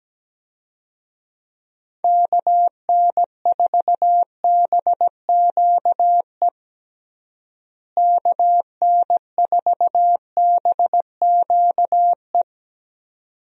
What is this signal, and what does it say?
Morse code, heard as a keyed tone, about 17 wpm, 700 Hz. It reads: KN4BQE KN4BQE
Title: USA Call Signs - 6 Characters - Repeated 2x - 17wpm